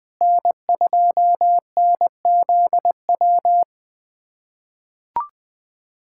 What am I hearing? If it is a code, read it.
Morse code, heard as a keyed tone, about 20 wpm, 700 Hz. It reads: N2NZW
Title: USA Callsign DE Callsign - 5 Characters - 20wpm